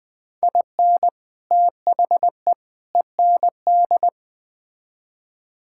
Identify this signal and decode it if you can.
Morse code, heard as a keyed tone, about 20 wpm, 700 Hz. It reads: IN THE END
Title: Build A Phrase - 3-Word Phrases - 20wpm